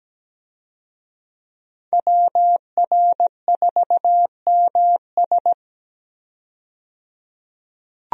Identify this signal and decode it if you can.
Morse code, heard as a keyed tone, about 17 wpm, 700 Hz. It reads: WR4MS
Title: USA Call Signs - 5 Characters - 17wpm